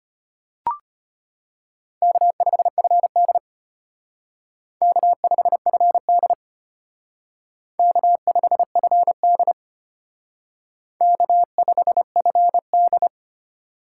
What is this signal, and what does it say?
Morse code, heard as a keyed tone, about 30 wpm, 700 Hz. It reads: K5FD K5FD K5FD K5FD
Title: Speed Racing - USA 1x2 Call Signs 25wpm